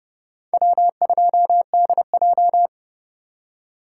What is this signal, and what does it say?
Morse code, heard as a keyed tone, about 28 wpm, 700 Hz. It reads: W2DJ